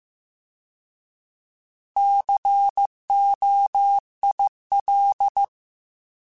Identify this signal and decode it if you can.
Morse code, heard as a keyed tone, about 15 wpm, 800 Hz. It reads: COIL
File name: Alphabetical / Alphabetical mp3s / Cs @13